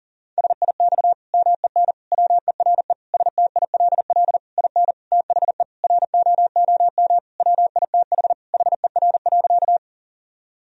Morse code, heard as a keyed tone, about 40 wpm, 700 Hz. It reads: SIX MEN WERE STILL IN THE ROOM WITH HER.
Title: Sentences from Top 1000 Words - Review Part 2 - 40wpm